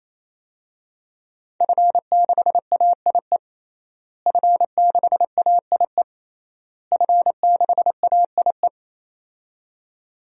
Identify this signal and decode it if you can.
Morse code, heard as a keyed tone, about 28 wpm, 700 Hz. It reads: F6AIE F6AIE F6AIE
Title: DXCC-100 Call Signs - 5 Characters - Repeated 3x - 28wpm